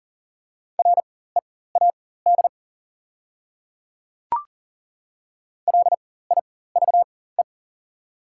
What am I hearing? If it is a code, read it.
Morse code, heard as a keyed tone, about 40 wpm, spaced as for 19 wpm, 690 Hz. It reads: READ LIVE
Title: 4x Character Spacing - Top 300 Words - 40wpm